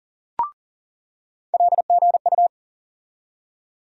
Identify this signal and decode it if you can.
Morse code, heard as a keyed tone, about 40 wpm, 700 Hz. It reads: LGU
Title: USA Call Sign Postfixes - 3 Letters 40wpm